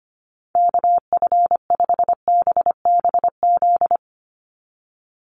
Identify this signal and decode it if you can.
Morse code, heard as a keyed tone, about 25 wpm, 700 Hz. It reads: KF5BBZ
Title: USA Call Signs 25wpm